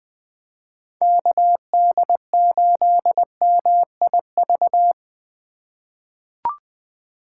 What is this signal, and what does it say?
Morse code, heard as a keyed tone, about 20 wpm, 700 Hz. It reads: KD8MIV
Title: 2023 POTA Hunter Call Signs - 20wpm